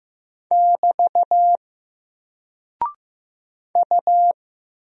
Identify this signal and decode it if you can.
Morse code, heard as a keyed tone, about 15 wpm, 700 Hz. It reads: <BT> U
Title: Single Character (Letter:Number:Punctuation:Prosign) Alt 15wpm